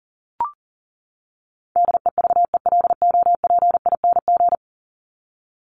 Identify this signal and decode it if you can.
Morse code, heard as a keyed tone, about 40 wpm, 700 Hz. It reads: DEVELOPING
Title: ING Suffix - Top 100 Words with Base Words 40wpm